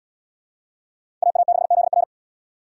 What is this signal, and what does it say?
Morse code, heard as a keyed tone, about 71 wpm, 700 Hz. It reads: AA6ZR